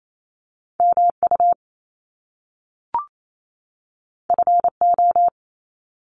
Morse code, heard as a keyed tone, about 28 wpm, 700 Hz. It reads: MU FO
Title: Top Two Letter Combinations in Words 28wpm